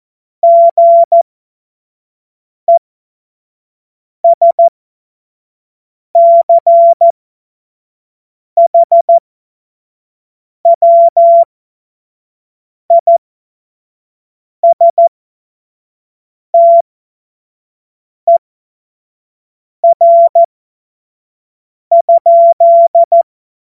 Morse code, heard as a keyed tone, about 14 wpm, spaced as for 5 wpm, 680 Hz. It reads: GESCHWISTER?
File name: Morsekurs / Witze / Witz 13. Januar 2026 25 F70